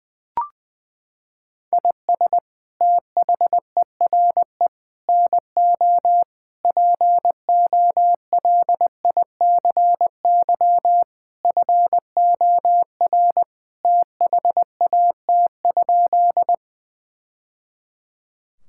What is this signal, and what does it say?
Morse code, heard as a keyed tone, about 20 wpm, 700 Hz. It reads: IS THERE NO POLICY FOR THAT?
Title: Sentences from Top 1000 Words - Review Part 5 - 20wpm